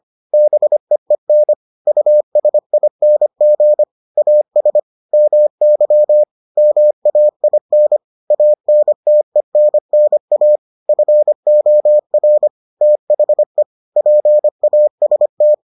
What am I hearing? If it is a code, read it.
Morse code, heard as a keyed tone, about 25 wpm, 600 Hz. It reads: BEEN USING AS MY MAIN ANTENNA FOR THE PAST